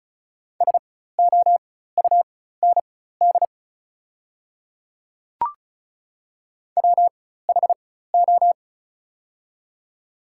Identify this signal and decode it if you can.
Morse code, heard as a keyed tone, about 36 wpm, spaced as for 16 wpm, 690 Hz. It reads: SOUND WHO
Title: 4x Character Spacing - Top 100 Words - 35wpm